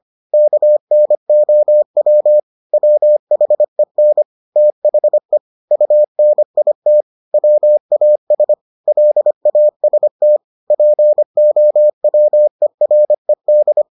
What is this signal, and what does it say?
Morse code, heard as a keyed tone, about 25 wpm, 600 Hz. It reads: KNOW WHEN THE UNIT WAS LAST POWERED